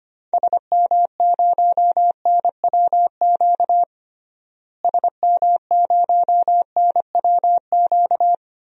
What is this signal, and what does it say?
Morse code, heard as a keyed tone, about 25 wpm, 700 Hz. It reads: SM0NWQ SM0NWQ
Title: DXCC-100 Call Signs - 6 Characters - Repeated 3x - 25wpm